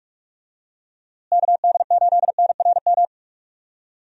Morse code, heard as a keyed tone, about 47 wpm, 700 Hz. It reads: KD8NRM
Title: USA Call Signs - 6 Characters - Repeated 2x - 45wpm